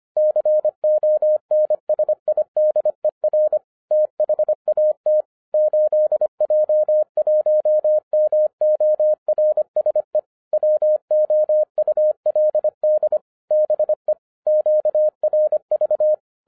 Morse code, heard as a keyed tone, about 25 wpm, 610 Hz. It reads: CONSIDER THAT 8J1MORSE WOULD BE QRV